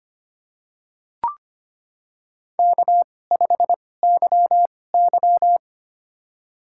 Morse code, heard as a keyed tone, about 25 wpm, spaced as for 18 wpm, 700 Hz. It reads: K5YY
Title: USA Call Signs - 2x Character Spacing - 25wpm